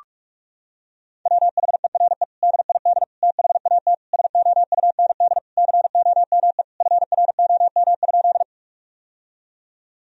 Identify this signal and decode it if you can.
Morse code, heard as a keyed tone, about 42 wpm, 700 Hz. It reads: WHERE DID THAT SOUND COME FROM?
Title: Sentences from Top 1000 Words - Review Part 1 - 45wpm